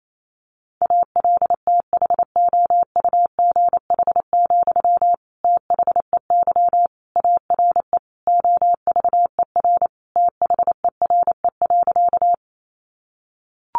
Morse code, heard as a keyed tone, about 28 wpm, 700 Hz. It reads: ALTHOUGH, THEY ARE OVER THERE.
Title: Sentences from Top 600 Words 28wpm